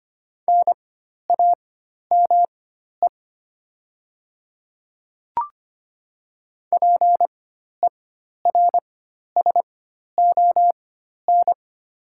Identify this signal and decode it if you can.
Morse code, heard as a keyed tone, about 25 wpm, spaced as for 12 wpm, 700 Hz. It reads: NAME PERSON
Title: 4x Character Spacing - Top 500 Words - 25wpm